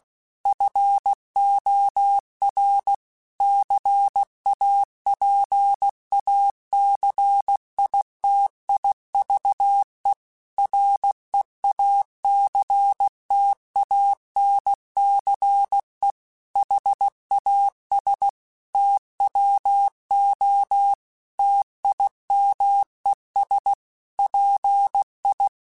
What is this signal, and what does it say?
Morse code, heard as a keyed tone, about 16 wpm, 800 Hz. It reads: FOR CAPACITIVE REACTANCE HAS TWO TIMES PI